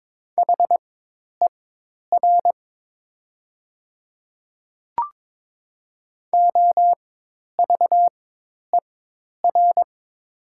Morse code, heard as a keyed tone, about 21 wpm, spaced as for 10 wpm, 700 Hz. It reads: HER OVER
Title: 4x Character Spacing - Top 100 Words - 22wpm